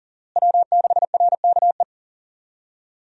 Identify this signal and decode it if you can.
Morse code, heard as a keyed tone, about 40 wpm, 700 Hz. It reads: W6RKE
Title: USA Call Signs - 5 Characters - Repeated 3x - 40wpm